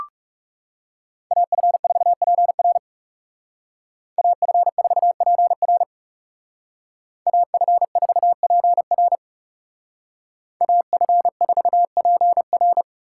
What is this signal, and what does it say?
Morse code, heard as a keyed tone, about 37 wpm, 700 Hz. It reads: AF4PR AF4PR AF4PR AF4PR
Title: Speed Racing - USA Call Signs 30wpm